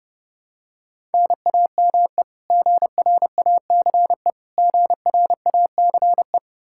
Morse code, heard as a keyed tone, about 28 wpm, 700 Hz. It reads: NAME GRACE GRACE
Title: Practice QSO - Exchange 1 - Beginning - 30wpm